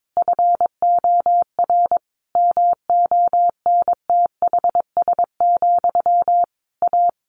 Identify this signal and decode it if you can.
Morse code, heard as a keyed tone, about 22 wpm, 700 Hz. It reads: FOR MONTHS, A
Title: A Sound Of Thunder 22wpm